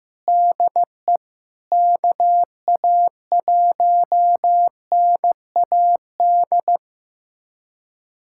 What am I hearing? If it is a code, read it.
Morse code, heard as a keyed tone, about 16 wpm, 700 Hz. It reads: DE KA1NAD
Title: USA Callsign DE Callsign - 6 Characters - 15wpm